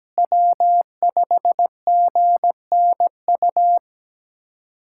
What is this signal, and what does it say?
Morse code, heard as a keyed tone, about 17 wpm, 700 Hz. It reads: W5GNU
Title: USA Callsign DE Callsign - 5 Characters - 17wpm